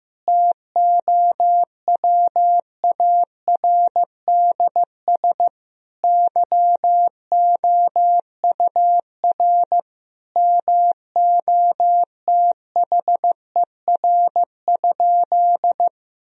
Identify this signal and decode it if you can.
Morse code, heard as a keyed tone, about 15 wpm, 700 Hz. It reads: TOWARDS YOUR MOTHER?
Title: Sentences from Top 1000 Words - Review Part 2 - 15wpm